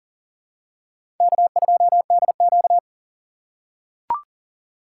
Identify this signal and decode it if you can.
Morse code, heard as a keyed tone, about 40 wpm, 700 Hz. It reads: K2DQ
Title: USA Call Signs - 4 Characters - 2x Character Spacing - 40wpm